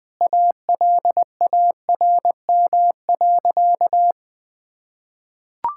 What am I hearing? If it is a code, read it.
Morse code, heard as a keyed tone, about 20 wpm, 700 Hz. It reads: ALARM.
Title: Sentences with 8-Word Phrases - Spoken Beginning - 20wpm